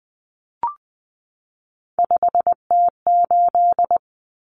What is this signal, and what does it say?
Morse code, heard as a keyed tone, about 20 wpm, 700 Hz. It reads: 5T8
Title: Common Cut Numbers - 3-Digit 20wpm